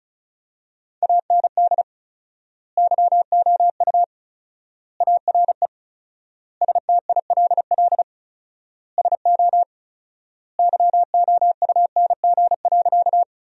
Morse code, heard as a keyed tone, about 34 wpm, 700 Hz. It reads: AND YOU ARE STILL SO YOUNG.
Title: 4x Word-Spacing - Sentences from Top 400 Words - 35wpm